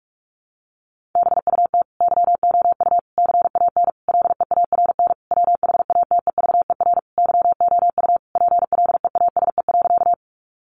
Morse code, heard as a keyed tone, about 44 wpm, 700 Hz. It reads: BUT YOU CAN LEARN WHATEVER YOU PLEASE.
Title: Sentences from Top 1500 Words 45wpm